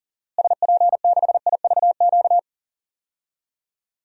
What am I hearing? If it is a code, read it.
Morse code, heard as a keyed tone, about 37 wpm, 700 Hz. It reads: SP6IVQ
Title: DXCC-100 Call Signs - 6 Characters - Repeated 2x - 40wpm